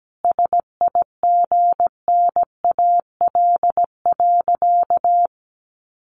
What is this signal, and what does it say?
Morse code, heard as a keyed tone, about 17 wpm, 700 Hz. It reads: SIGNAL.
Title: Sentences from Top 1400 Words 17wpm